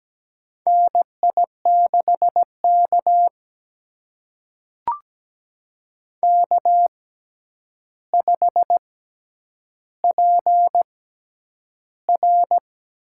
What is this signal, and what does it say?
Morse code, heard as a keyed tone, about 17 wpm, spaced as for 9 wpm, 700 Hz. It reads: NI6K K5PR
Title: USA Call Signs - 4 Characters - 6x Character Spacing - 17wpm